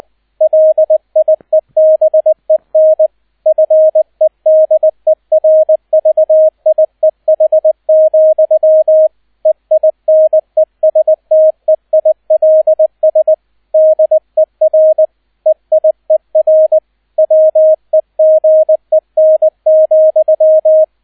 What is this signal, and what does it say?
Morse code, heard as a keyed tone, about 20 wpm, 630 Hz, with radio noise. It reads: LIEBEN FEDERVIEH, EINESTEILS DER EIER WEGEN,